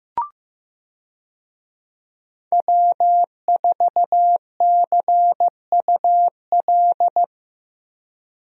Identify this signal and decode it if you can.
Morse code, heard as a keyed tone, about 15 wpm, 700 Hz. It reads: W4CUL
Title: USA Call Signs - 5 Characters - Repeated 2x - 15wpm